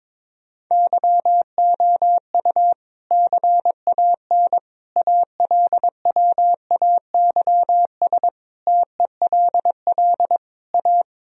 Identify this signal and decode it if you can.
Morse code, heard as a keyed tone, about 22 wpm, 700 Hz. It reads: YOU CAN ALWAYS TELL A
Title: Sentences from Top 1000 Words - Review Part 4 - 22wpm